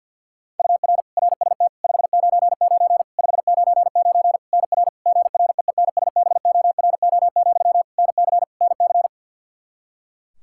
Morse code, heard as a keyed tone, about 51 wpm, 700 Hz. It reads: UR RST 599 599 NR GREENSBORO, NC NC
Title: Practice QSO - Exchange 1 - Beginning - 50wpm